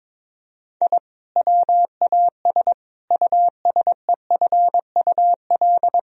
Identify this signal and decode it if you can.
Morse code, heard as a keyed tone, about 22 wpm, 700 Hz. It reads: I WAS USEFUL
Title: Sentences from Top 1000 Words - Review Part 4 - 22wpm